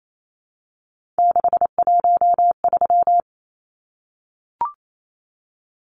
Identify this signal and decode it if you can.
Morse code, heard as a keyed tone, about 28 wpm, 700 Hz. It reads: 613
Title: Numbers - 3-Digit 28wpm